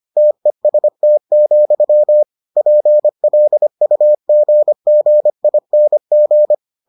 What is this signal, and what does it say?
Morse code, heard as a keyed tone, about 25 wpm, 600 Hz. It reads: TEST, PLUGGING